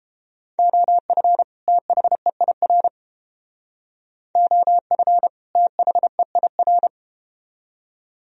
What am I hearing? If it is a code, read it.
Morse code, heard as a keyed tone, about 31 wpm, 700 Hz. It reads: OF THEIR OF THEIR